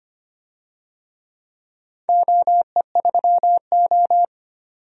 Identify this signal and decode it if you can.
Morse code, heard as a keyed tone, about 25 wpm, 700 Hz. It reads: OE3O
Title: DXCC-100 Call Signs - 4 Characters - 25wpm